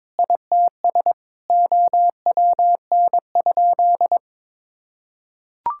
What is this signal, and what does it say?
Morse code, heard as a keyed tone, about 22 wpm, 700 Hz. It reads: ITS OWN?
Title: Sentences from Top 1000 Words - Review Part 5 - 22wpm